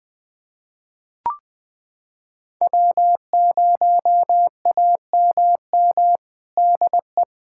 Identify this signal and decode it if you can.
Morse code, heard as a keyed tone, about 20 wpm, 700 Hz. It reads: W0AMM DE
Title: USA Callsign DE Callsign 20wpm